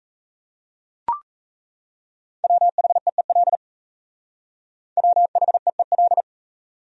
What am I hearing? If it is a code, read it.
Morse code, heard as a keyed tone, about 34 wpm, 700 Hz. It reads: WHEEL WHEEL
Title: Speed-Racing Top 500 Words 28wpm